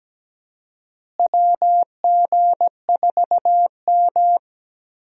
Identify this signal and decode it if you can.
Morse code, heard as a keyed tone, about 17 wpm, 700 Hz. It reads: WG4M
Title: USA Call Signs - 4 Characters - Repeated 3x - 17wpm